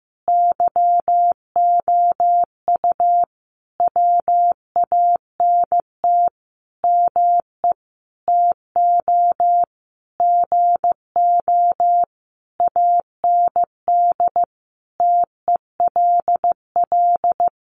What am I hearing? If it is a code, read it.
Morse code, heard as a keyed tone, about 15 wpm, 700 Hz. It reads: YOU WANT ME TO GO AND TELL
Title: Sentences from Top 200 Words 15wpm